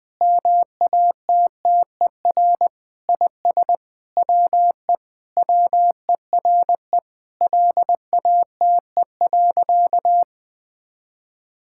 Morse code, heard as a keyed tone, about 20 wpm, 700 Hz. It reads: MATTER IS WE WERE LATE.
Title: Sentences with 5-Word Phrases - 20wpm